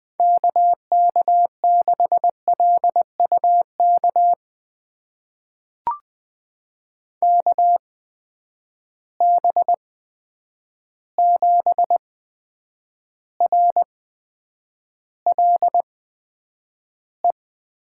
Morse code, heard as a keyed tone, about 20 wpm, spaced as for 9 wpm, 700 Hz. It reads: KK6LUK KB7RLE
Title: USA Call Signs - 6 Characters - 8x Character Spacing - 20wpm